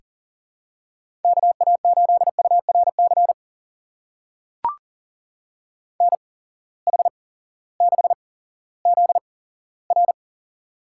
Morse code, heard as a keyed tone, about 40 wpm, spaced as for 18 wpm, 700 Hz. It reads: KA8URC NH6ZR
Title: USA Call Signs - 8x Character Spacing - 40wpm